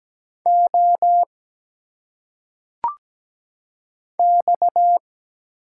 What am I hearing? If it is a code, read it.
Morse code, heard as a keyed tone, about 17 wpm, 700 Hz. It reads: O X
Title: Single Letters - Phonetic Pronunciation 17wpm